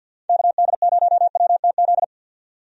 Morse code, heard as a keyed tone, about 50 wpm, 700 Hz. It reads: KD0WTZ
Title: USA Call Signs - 6 Characters - 50wpm